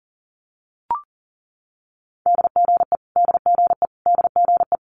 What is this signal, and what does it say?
Morse code, heard as a keyed tone, about 40 wpm, 700 Hz. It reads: DGE DGE DGE
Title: Multi-letter Phonemes Repeated 3x 40wpm